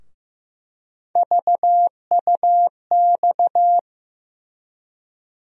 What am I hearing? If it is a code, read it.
Morse code, heard as a keyed tone, about 15 wpm, 700 Hz. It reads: VUX
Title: USA Call Sign Postfixes - 3 Letters 15wpm